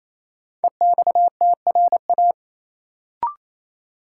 Morse code, heard as a keyed tone, about 28 wpm, 700 Hz. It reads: EXTRA
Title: Sentences from Top 1000 Words - New Word Drill - 28wpm